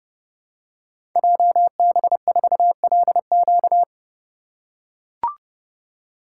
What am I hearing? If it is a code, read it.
Morse code, heard as a keyed tone, about 30 wpm, 700 Hz. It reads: JB4LQ
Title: Code Groups - 5 Letter:Numbers - 30wpm